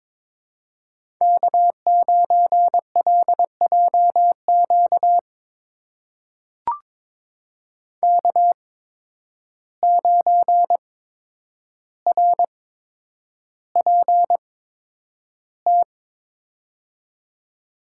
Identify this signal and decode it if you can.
Morse code, heard as a keyed tone, about 22 wpm, spaced as for 9 wpm, 700 Hz. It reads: K9LJQ K9RPT
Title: USA Call Signs - 8x Character Spacing - 22wpm